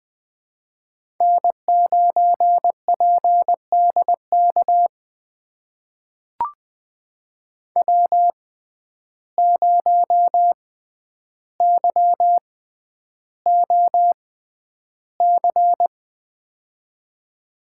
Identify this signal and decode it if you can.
Morse code, heard as a keyed tone, about 20 wpm, spaced as for 10 wpm, 700 Hz. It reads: N9PDK W0YOC
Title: USA Call Signs - 6x Character Spacing - 20wpm